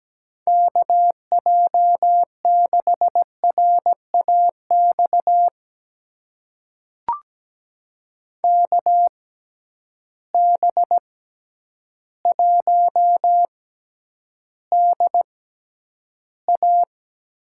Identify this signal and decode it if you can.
Morse code, heard as a keyed tone, about 17 wpm, spaced as for 9 wpm, 700 Hz. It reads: KJ6RAX KB1DA
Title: USA Call Signs - 6 Characters - 6x Character Spacing - 17wpm